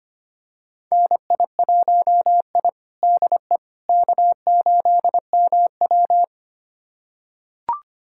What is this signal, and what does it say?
Morse code, heard as a keyed tone, about 25 wpm, 700 Hz. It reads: NI1I DE K8MW
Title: USA Callsign DE Callsign - 4 Characters - 25wpm